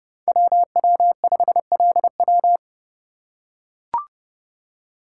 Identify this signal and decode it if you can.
Morse code, heard as a keyed tone, about 31 wpm, 700 Hz. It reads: WW5LW
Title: USA Call Signs - 5 Characters - 2x Character Spacing - 30wpm